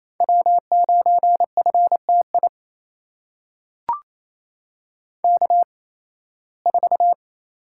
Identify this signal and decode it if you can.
Morse code, heard as a keyed tone, about 28 wpm, spaced as for 18 wpm, 700 Hz. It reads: W9FTI K4
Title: USA Call Signs - 5 Characters - 8x Character Spacing - 28wpm